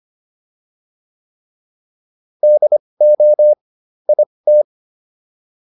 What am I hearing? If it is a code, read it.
Morse code, heard as a keyed tone, about 25 wpm, spaced as for 20 wpm, 600 Hz. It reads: DO IT